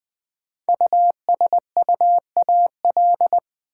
Morse code, heard as a keyed tone, about 20 wpm, 700 Hz. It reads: USUAL
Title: Sentences from Top 400 Words - New Word Drill - 20wpm